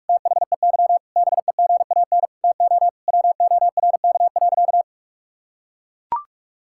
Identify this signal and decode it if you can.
Morse code, heard as a keyed tone, about 41 wpm, 700 Hz. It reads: THEY BEGAN TO WORK.
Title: Sentences from Top 1000 Words - Review Part 6 - 45wpm